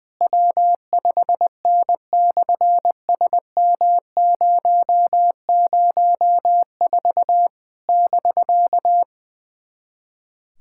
Morse code, heard as a keyed tone, about 20 wpm, 700 Hz. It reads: W5N/SM004 <BK>
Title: SOTA QSO Practice 20wpm